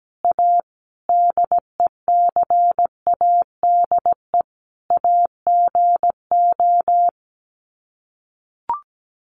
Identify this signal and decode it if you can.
Morse code, heard as a keyed tone, about 17 wpm, 700 Hz. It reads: A DECADE AGO
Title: Sets of 5 Words Encore2 17wpm